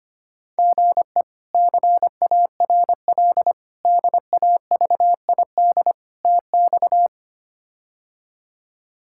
Morse code, heard as a keyed tone, about 25 wpm, 700 Hz. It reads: GE CARL DAVID TX
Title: K1USN SST QSO Practice - 25wpm